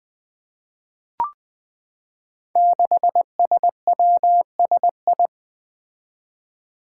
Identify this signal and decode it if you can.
Morse code, heard as a keyed tone, about 20 wpm, 700 Hz. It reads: 6SWSI
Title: Five Challenging Letter-Number Combinations 20wpm